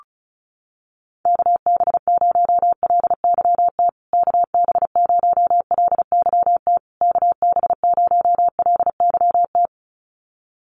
Morse code, heard as a keyed tone, about 34 wpm, 700 Hz. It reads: KB0LYT KB0LYT KB0LYT
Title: USA Call Signs Repeated 3x 35wpm